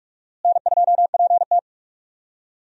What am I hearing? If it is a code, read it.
Morse code, heard as a keyed tone, about 42 wpm, 700 Hz. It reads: N2PT
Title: USA Call Signs - 4 Characters - Repeated 2x - 45wpm